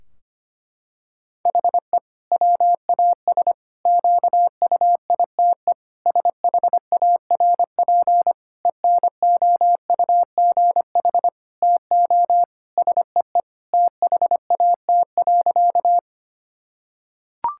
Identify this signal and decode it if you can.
Morse code, heard as a keyed tone, about 25 wpm, 700 Hz. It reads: HE WAS QUITE SHARP ENOUGH TO SEE THAT.
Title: Sentences from Top 1600 Words 25wpm